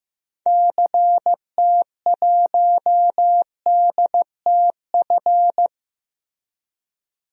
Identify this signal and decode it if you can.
Morse code, heard as a keyed tone, about 16 wpm, 700 Hz. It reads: CT1DTF
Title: DXCC-100 Call Signs - 6 Characters - Repeated 2x - 15wpm